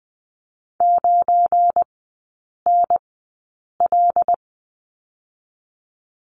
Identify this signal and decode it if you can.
Morse code, heard as a keyed tone, about 20 wpm, spaced as for 8 wpm, 690 Hz. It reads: 9NL
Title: 205 Lesson 33e - Letter j - Three Characters 20wpm